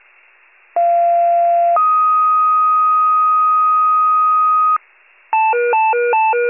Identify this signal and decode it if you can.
Two-tone page, heard as a one-second tone, then a three-second tone, about 690 then 1250 Hz. It is heard from 0.8 to 4.8 s.